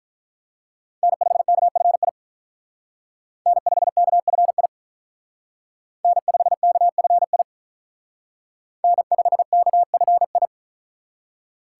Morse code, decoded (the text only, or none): N5KFI N5KFI N5KFI N5KFI